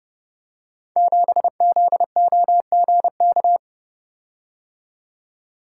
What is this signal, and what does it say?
Morse code, heard as a keyed tone, about 30 wpm, 700 Hz. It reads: 7ZOGK
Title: Code Groups - 5 Letter:Numbers - 30wpm